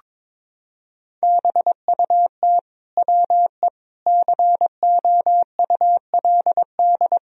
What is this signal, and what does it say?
Morse code, heard as a keyed tone, about 22 wpm, 700 Hz. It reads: BUT WE COULD
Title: Sentences from Top 1000 Words - Review Part 2 - 22wpm